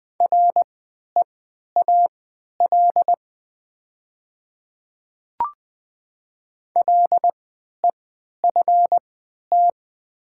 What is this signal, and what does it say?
Morse code, heard as a keyed tone, about 20 wpm, spaced as for 11 wpm, 700 Hz. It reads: REAL LEFT
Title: 3x Character Spacing - Top 300 Words - 20wpm